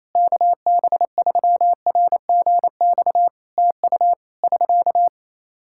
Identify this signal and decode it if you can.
Morse code, heard as a keyed tone, about 28 wpm, 700 Hz. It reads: KB3RGX TU <SK>
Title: Practice QSO - Exchange 4 - Ending - 28wpm